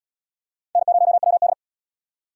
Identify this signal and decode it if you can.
Morse code, heard as a keyed tone, about 77 wpm, 700 Hz. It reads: N0WD